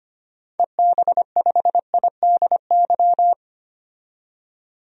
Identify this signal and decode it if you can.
Morse code, heard as a keyed tone, about 25 wpm, 690 Hz. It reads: EB5IDY
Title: DXCC-100 Call Signs - 6 Characters - 25wpm